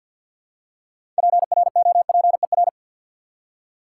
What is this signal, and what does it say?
Morse code, heard as a keyed tone, about 53 wpm, 700 Hz. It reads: PROPER
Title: Six Letter Words - 50wpm